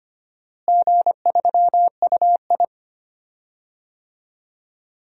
Morse code, heard as a keyed tone, about 25 wpm, 700 Hz. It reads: G3UI
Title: DXCC-100 Call Signs - 4 Characters - Repeated 2x - 25wpm